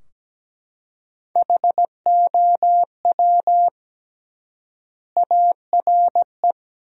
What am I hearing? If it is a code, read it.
Morse code, heard as a keyed tone, about 17 wpm, 700 Hz. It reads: HOW ARE
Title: 3x Word-Spacing - Sentences from Top 100 Words - 17wpm